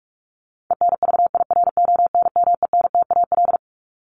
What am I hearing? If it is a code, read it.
Morse code, heard as a keyed tone, about 44 wpm, 700 Hz. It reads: ENVIRONMENTAL
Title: Top 1001-1250 Words 45wpm